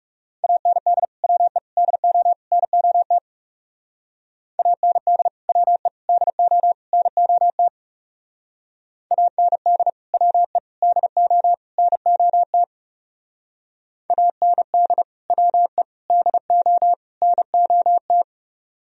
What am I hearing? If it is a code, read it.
Morse code, heard as a keyed tone, about 37 wpm, 690 Hz. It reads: AND WE DO NOT AND WE DO NOT AND WE DO NOT AND WE DO NOT
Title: Speed-Racing - Sets of 4 Words 30wpm